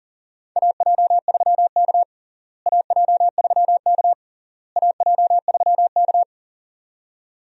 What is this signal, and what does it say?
Morse code, heard as a keyed tone, about 40 wpm, 700 Hz. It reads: AJ3K AJ3K AJ3K
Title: USA Call Signs - 4 Characters - Repeated 3x - 40wpm